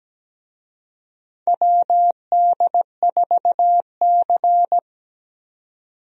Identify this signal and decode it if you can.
Morse code, heard as a keyed tone, about 17 wpm, 700 Hz. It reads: WD4C
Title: USA Call Signs - 4 Characters - 17wpm